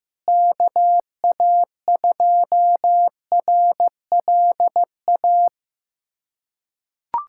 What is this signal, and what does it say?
Morse code, heard as a keyed tone, about 15 wpm, 700 Hz. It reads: KA2RLA
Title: USA Callsign DE Callsign 15wpm